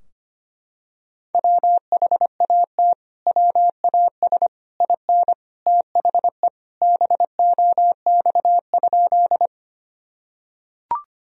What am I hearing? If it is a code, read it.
Morse code, heard as a keyed tone, about 25 wpm, 700 Hz. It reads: WHAT WAS IN THE BOX?
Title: Sentences from Top 1000 Words - Review Part 2 - 25wpm